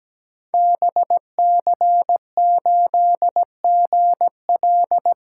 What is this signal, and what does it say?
Morse code, heard as a keyed tone, about 17 wpm, 700 Hz. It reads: BC8GL
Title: Code Groups - 5 Letter:Numbers - 17wpm